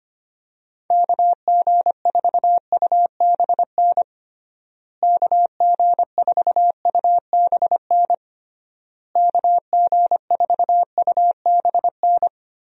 Morse code, heard as a keyed tone, about 25 wpm, 700 Hz. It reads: KG4UBN KG4UBN KG4UBN
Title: USA Call Signs - 6 Characters - Repeated 3x - 25wpm